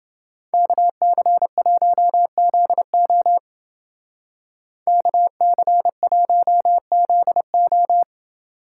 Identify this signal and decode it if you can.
Morse code, heard as a keyed tone, about 29 wpm, 700 Hz. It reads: KC1ZO KC1ZO